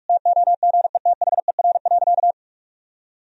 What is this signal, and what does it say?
Morse code, heard as a keyed tone, about 46 wpm, 700 Hz. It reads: TOGETHER.
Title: Sentences from Top 1000 Words - Review Part 2 - 45wpm